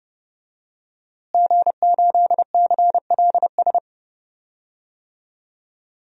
Morse code, heard as a keyed tone, about 30 wpm, 700 Hz. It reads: G8CLS
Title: DXCC-100 Call Signs - 5 Characters - Repeated 3x - 30wpm